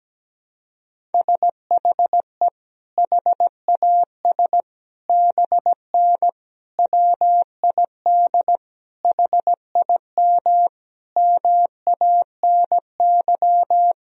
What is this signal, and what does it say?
Morse code, heard as a keyed tone, about 17 wpm, 700 Hz. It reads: SHE HAS BN WID HIM MANY
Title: Sentences from Top 200 Words and Abbreviations 17wpm